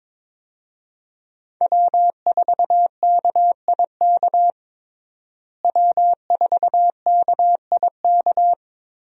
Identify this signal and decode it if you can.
Morse code, heard as a keyed tone, about 22 wpm, 700 Hz. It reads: W4KIK W4KIK
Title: USA Call Signs - 5 Characters - Repeated 2x - 22wpm